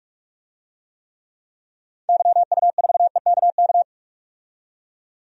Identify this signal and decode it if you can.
Morse code, heard as a keyed tone, about 47 wpm, 700 Hz. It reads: YU4EKK
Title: DXCC-100 Call Signs - 6 Characters - 45wpm